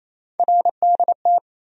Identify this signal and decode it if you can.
Morse code, heard as a keyed tone, about 28 wpm, 700 Hz. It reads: RDT
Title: USA Call Sign Postfixes - 3 Letters 28wpm